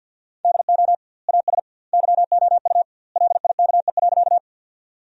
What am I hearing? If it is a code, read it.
Morse code, heard as a keyed tone, about 50 wpm, 700 Hz. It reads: DO AS YOU LIKE.
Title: 2x Word-Spacing - Sentences from Top 100 Words - 50wpm